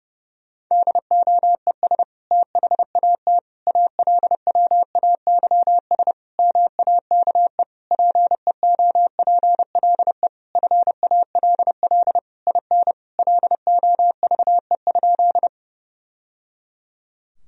Morse code, heard as a keyed tone, about 30 wpm, 700 Hz. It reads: DOES THAT ALWAYS MAKE PEOPLE FALL IN LOVE?
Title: Sentences from Top 1000 Words - Review Part 2 - 30wpm